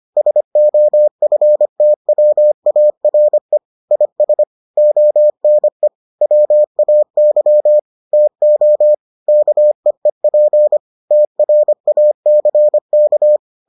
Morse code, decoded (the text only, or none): SOFTWARE IS ONE WAY TO KEEP TRACK